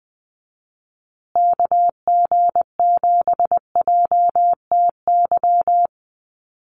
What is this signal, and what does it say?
Morse code, heard as a keyed tone, about 20 wpm, 700 Hz. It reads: KG7JTY
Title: USA Call Signs Repeated 2x 20wpm